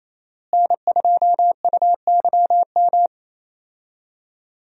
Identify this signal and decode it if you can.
Morse code, heard as a keyed tone, about 28 wpm, 700 Hz. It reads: N2UYM
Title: USA Call Signs - 5 Characters - Repeated 2x - 28wpm